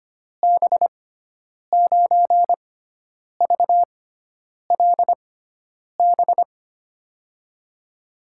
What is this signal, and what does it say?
Morse code, heard as a keyed tone, about 25 wpm, spaced as for 9 wpm, 700 Hz. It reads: B9VLB
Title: USA Call Signs - 6 Characters - 6x Character Spacing - 25wpm